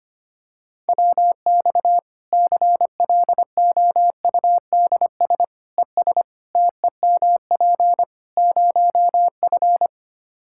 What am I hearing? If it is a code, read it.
Morse code, heard as a keyed tone, about 25 wpm, 700 Hz. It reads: WX CLOUDS ES TEMP 0F
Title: Practice QSO - Exchange 2 - Info - 25wpm